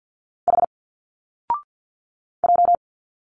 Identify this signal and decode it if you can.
Morse code, heard as a keyed tone, about 50 wpm, 700 Hz. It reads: H J
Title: Single Letters 50wpm